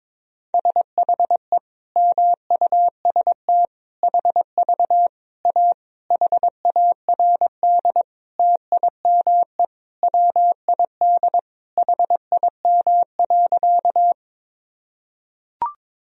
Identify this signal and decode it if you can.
Morse code, heard as a keyed tone, about 22 wpm, 700 Hz. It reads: SHE MUST HV A HARD TIME WID HIM.
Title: Sentences from Top 300 Words and Abbreviations 22wpm